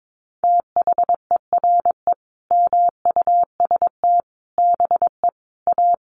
Morse code, heard as a keyed tone, about 22 wpm, 700 Hz. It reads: THERE MUST BE A
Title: Sentences from Top 1400 Words 22wpm